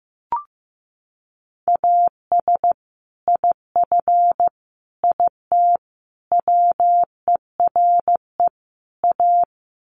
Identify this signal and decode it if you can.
Morse code, heard as a keyed tone, about 15 wpm, 700 Hz. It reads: AS IF IT WERE A
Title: Sets of 5 Words 15wpm